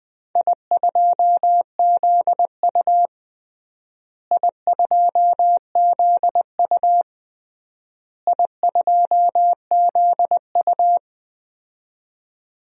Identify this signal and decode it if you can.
Morse code, heard as a keyed tone, about 20 wpm, 700 Hz. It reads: I2ZU I2ZU I2ZU
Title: DXCC-100 Call Signs - 4 Characters - Repeated 3x - 20wpm